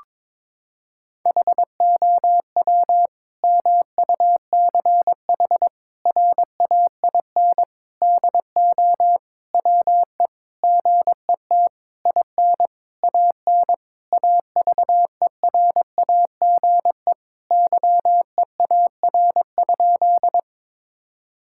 Morse code, heard as a keyed tone, about 22 wpm, 700 Hz. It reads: HOW MUCH RAIN DO WE GET IN AN AVERAGE YEAR?
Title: Sentences from Top 1000 Words - Review Part 6 - 22wpm